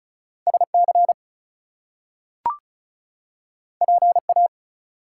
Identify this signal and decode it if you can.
Morse code, heard as a keyed tone, about 36 wpm, 700 Hz. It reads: SC PA
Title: US States and Canadian Provinces 35wpm